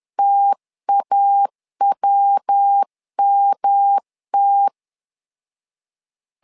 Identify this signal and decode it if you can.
Morse code, heard as a keyed tone, about 10 wpm, 800 Hz. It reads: TAWMT
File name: LEZIONI / 3 MWRO / 3 MW 10wpm 1959